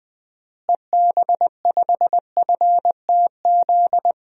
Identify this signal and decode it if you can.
Morse code, heard as a keyed tone, about 20 wpm, 700 Hz. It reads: EB5FTZ
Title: DXCC-100 Call Signs - 6 Characters - Repeated 2x - 20wpm